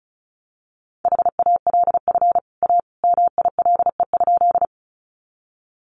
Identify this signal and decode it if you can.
Morse code, heard as a keyed tone, about 34 wpm, 700 Hz. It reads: HALF A MILE?
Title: Sentences from Top 400 Words 35wpm